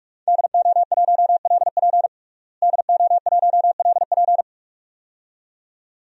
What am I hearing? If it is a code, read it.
Morse code, heard as a keyed tone, about 45 wpm, 700 Hz. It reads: DO1LP DO1LP
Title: DXCC-100 Call Signs - 5 Characters - Repeated 2x - 45wpm